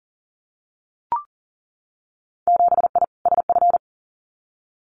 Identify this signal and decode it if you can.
Morse code, heard as a keyed tone, about 38 wpm, 700 Hz. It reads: 7I SF
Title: Winter Field Day Exchanges Encore 40wpm